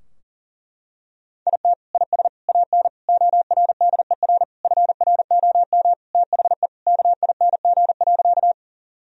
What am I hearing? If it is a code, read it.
Morse code, heard as a keyed tone, about 40 wpm, 700 Hz. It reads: IT IS AN ORDER FROM THE KING.
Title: Sentences from Top 1000 Words - Review Part 2 - 40wpm